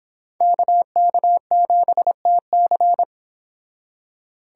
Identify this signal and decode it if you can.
Morse code, heard as a keyed tone, about 25 wpm, 700 Hz. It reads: KK7TC